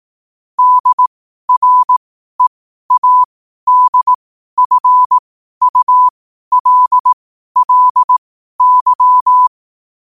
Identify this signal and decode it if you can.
Morse code, heard as a keyed tone, about 18 wpm, spaced as for 12 wpm, 1000 Hz. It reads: DREADFULLY
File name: qotdfiles / 12wpm